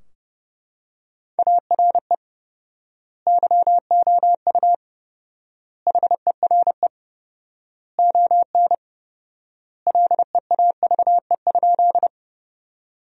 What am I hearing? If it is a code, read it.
Morse code, heard as a keyed tone, about 31 wpm, 700 Hz. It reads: ARE YOU HERE ON LEAVE?
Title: 4x Word-Spacing - Sentences from Top 400 Words - 30wpm